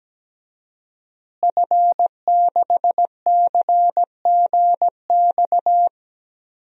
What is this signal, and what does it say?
Morse code, heard as a keyed tone, about 17 wpm, 700 Hz. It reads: F6CGX
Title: DXCC-100 Call Signs - 5 Characters - Repeated 3x - 17wpm